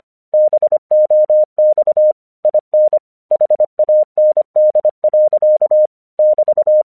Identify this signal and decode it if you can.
Morse code, heard as a keyed tone, about 25 wpm, 610 Hz. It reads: BOX IN HAND. <BT>
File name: 1 English 25wpm / AQC22